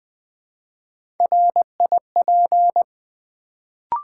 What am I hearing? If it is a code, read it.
Morse code, heard as a keyed tone, about 20 wpm, 700 Hz. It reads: RIP
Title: Build A Word - 1 Letter At A Time - 3-Letter Words - 20wpm